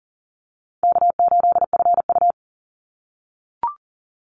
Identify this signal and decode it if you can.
Morse code, heard as a keyed tone, about 40 wpm, 700 Hz. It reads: K8FU
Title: Easy Call Signs - 40wpm